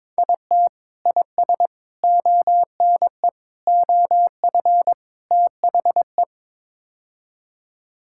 Morse code, heard as a keyed tone, about 22 wpm, 700 Hz. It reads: IT IS ONE OF THE
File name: practice / 22wpm / Build A Phrase - 5-Word Phrases - 22wpm